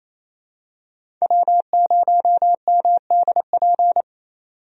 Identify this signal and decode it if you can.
Morse code, heard as a keyed tone, about 28 wpm, 700 Hz. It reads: W0MDP
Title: USA Call Signs - 5 Characters - 28wpm